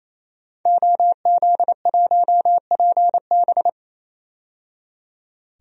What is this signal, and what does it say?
Morse code, heard as a keyed tone, about 28 wpm, 700 Hz. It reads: OZ1PB
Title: DXCC-100 Call Signs - 5 Characters - 28wpm